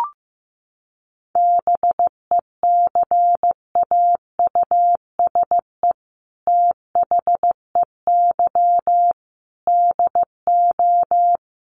Morse code, heard as a keyed tone, about 15 wpm, 700 Hz. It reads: BECAUSE THEY DO
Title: Sets of 5 Words Encore2 15wpm